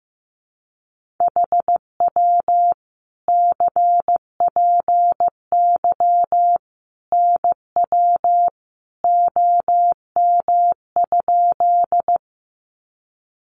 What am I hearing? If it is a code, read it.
Morse code, heard as a keyed tone, about 15 wpm, 700 Hz. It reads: HW CPY NW OM?
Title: Common QSO Phrases 15wpm